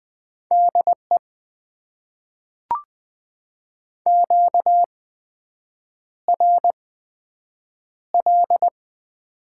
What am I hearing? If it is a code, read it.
Morse code, heard as a keyed tone, about 20 wpm, spaced as for 7 wpm, 700 Hz. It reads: DE QRL
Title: Top 90 QSO Elements - 8x Character Spacing - 20wpm